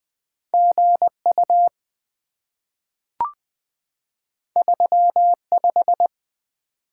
Morse code, heard as a keyed tone, about 20 wpm, 700 Hz. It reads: GU 35
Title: Two Letter-Number Combinations - 20wpm